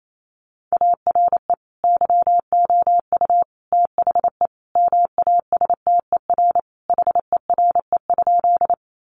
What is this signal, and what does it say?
Morse code, heard as a keyed tone, about 28 wpm, 700 Hz. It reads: ARE YOU THE MASTER HERE?
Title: Sentences from Top 1300 Words 28wpm